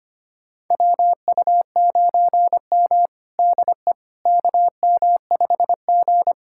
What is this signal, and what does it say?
Morse code, heard as a keyed tone, about 25 wpm, 700 Hz. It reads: WU9M DE KM5G
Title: USA Callsign DE Callsign - 4 Characters - 25wpm